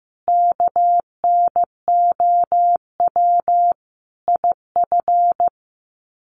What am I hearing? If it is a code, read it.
Morse code, heard as a keyed tone, about 15 wpm, 700 Hz. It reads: KNOW IF
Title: Sets of 5 Words 15wpm